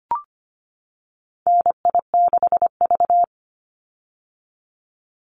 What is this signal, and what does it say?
Morse code, heard as a keyed tone, about 25 wpm, 700 Hz. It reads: NI6V
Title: Call Signs - 25wpm